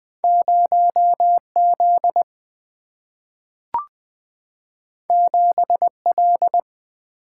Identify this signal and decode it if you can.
Morse code, heard as a keyed tone, about 20 wpm, 700 Hz. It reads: 0 Z 7 L
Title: Single-Letter Single-Number 20wpm